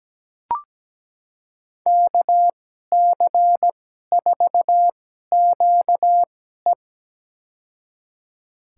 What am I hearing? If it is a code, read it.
Morse code, heard as a keyed tone, about 17 wpm, spaced as for 12 wpm, 700 Hz. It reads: KC4QE
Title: USA Call Signs - 5 Characters - 2x Character Spacing - 17wpm